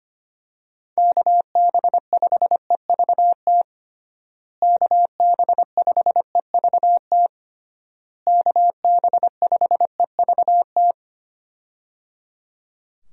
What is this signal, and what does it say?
Morse code, heard as a keyed tone, about 25 wpm, 700 Hz. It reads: KB5EVT KB5EVT KB5EVT
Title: USA Call Signs - 6 Characters - Repeated 3x - 25wpm